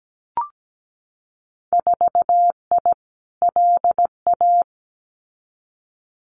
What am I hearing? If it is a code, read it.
Morse code, heard as a keyed tone, about 17 wpm, 700 Hz. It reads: 4I LA
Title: Winter Field Day Exchanges Encore 17wpm